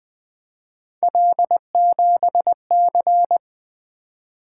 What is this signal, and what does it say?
Morse code, heard as a keyed tone, about 20 wpm, 700 Hz. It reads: L7C
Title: Three Letter-Number Combinations - 20wpm